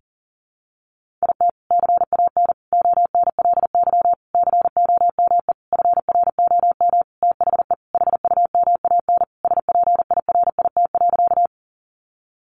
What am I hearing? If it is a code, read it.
Morse code, heard as a keyed tone, about 40 wpm, 700 Hz. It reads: IT CAN ONLY COME FROM THE HUMAN SPIRIT.
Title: Sentences from Top 1400 Words 40wpm